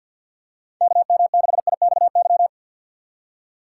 Morse code, heard as a keyed tone, about 51 wpm, 700 Hz. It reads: KM6IXY
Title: USA Call Signs - 3x Character Spacing - 50wpm